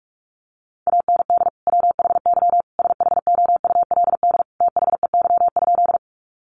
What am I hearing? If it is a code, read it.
Morse code, heard as a keyed tone, about 44 wpm, 700 Hz. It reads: AND WHY SHOULD THEY?
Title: Sentences from Top 300 Words 45wpm